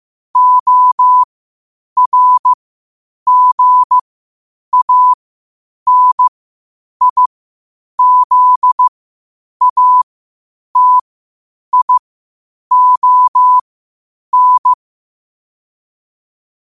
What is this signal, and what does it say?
Morse code, heard as a keyed tone, about 15 wpm, spaced as for 8 wpm, 1000 Hz. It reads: ORGANIZATION
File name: qotdfiles / 8wpm